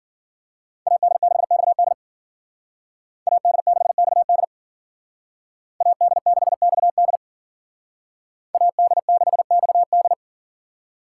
Morse code, heard as a keyed tone, about 52 wpm, 700 Hz. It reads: AD6XD AD6XD AD6XD AD6XD